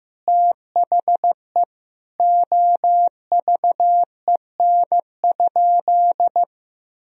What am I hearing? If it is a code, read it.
Morse code, heard as a keyed tone, about 15 wpm, 700 Hz. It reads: THE OVEN?
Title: Sentences from Top 1000 Words - Review Part 4 - 15wpm